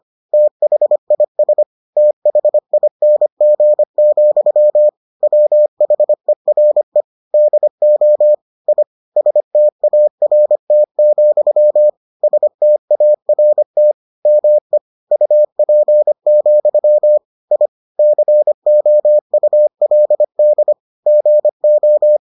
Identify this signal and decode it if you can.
Morse code, heard as a keyed tone, about 25 wpm, 600 Hz. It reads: THIS THING, WHERE DO I START, START ME UP, I COULD GO